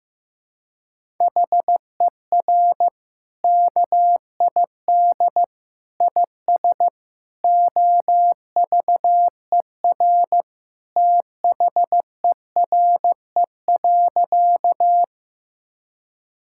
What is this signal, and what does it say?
Morse code, heard as a keyed tone, about 15 wpm, 700 Hz. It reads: HER KID IS OVER THERE.
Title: Sentences from Top 1600 Words 15wpm